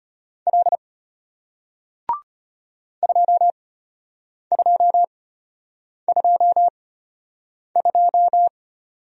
Morse code, heard as a keyed tone, about 34 wpm, spaced as for 8 wpm, 700 Hz. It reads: L 2222